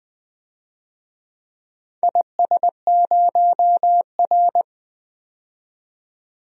Morse code, heard as a keyed tone, about 20 wpm, 700 Hz. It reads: IS0R
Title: DXCC-100 Call Signs - 4 Characters - 20wpm